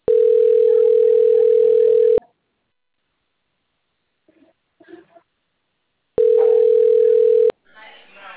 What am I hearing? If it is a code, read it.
Ringback tone (North America), 440 + 480 Hz.